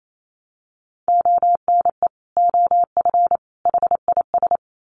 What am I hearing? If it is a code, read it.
Morse code, heard as a keyed tone, about 28 wpm, 700 Hz. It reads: ONE OF HIS
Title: Sets of 3 Words 28wpm